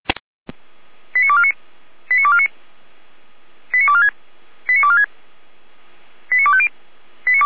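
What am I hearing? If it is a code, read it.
Five-tone selective call (ZVEI), heard as quick runs of five tones, tones of about 80 ms. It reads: 89249 89249 89246 89246 89250
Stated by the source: ZVEI five-tone call received by SDR, fire brigade Feuerwehr Orlamünde.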